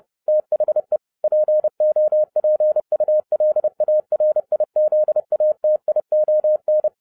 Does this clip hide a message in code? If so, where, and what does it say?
Yes: Morse code from 0.3 to 6.9 s, heard as a keyed tone, about 30 wpm, 610 Hz. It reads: THE POPULARIZATION